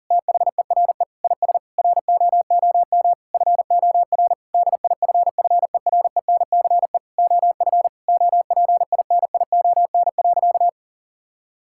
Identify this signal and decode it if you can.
Morse code, heard as a keyed tone, about 40 wpm, 700 Hz. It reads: THERE IS ROOM FOR DIFFERENCE OF OPINION.
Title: Sentences from Top 1600 Words 40wpm